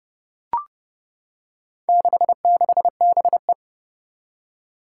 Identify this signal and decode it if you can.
Morse code, heard as a keyed tone, about 29 wpm, 700 Hz. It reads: 66BE
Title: Four Challenging Letter-Number Combinations 30wpm